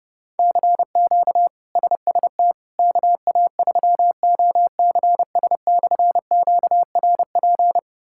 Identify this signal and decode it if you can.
Morse code, heard as a keyed tone, about 32 wpm, 700 Hz. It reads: CQ SST KA3OCS/QRP
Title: K1USN SST QSO Practice - 30wpm